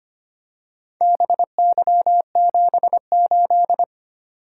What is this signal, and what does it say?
Morse code, heard as a keyed tone, about 25 wpm, 700 Hz. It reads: BY78
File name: practice / 25wpm / Four Letter-Number Combinations - 25wpm